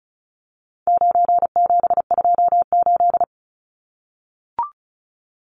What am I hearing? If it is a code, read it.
Morse code, heard as a keyed tone, about 34 wpm, 700 Hz. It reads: 9728
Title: Numbers - 4-Digit 35wpm